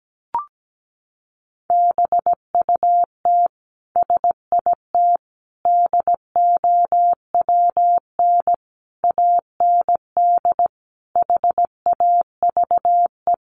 Morse code, heard as a keyed tone, about 17 wpm, 700 Hz. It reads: BUT SIT DOWN AND HAVE
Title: Sentences from Top 600 Words 17wpm